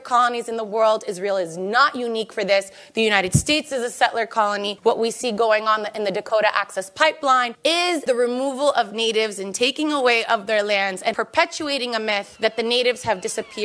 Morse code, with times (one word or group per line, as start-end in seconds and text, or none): none